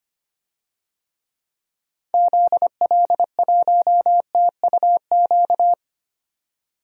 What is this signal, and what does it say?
Morse code, heard as a keyed tone, about 25 wpm, 700 Hz. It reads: ZL1TUQ
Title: DXCC-100 Call Signs - 6 Characters - 25wpm